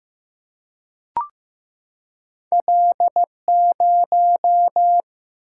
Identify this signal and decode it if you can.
Morse code, heard as a keyed tone, about 15 wpm, 700 Hz. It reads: L0
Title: Single-Letter Single-Number 15wpm